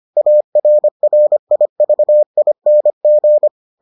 Morse code, heard as a keyed tone, about 25 wpm, 600 Hz. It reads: ARRIVING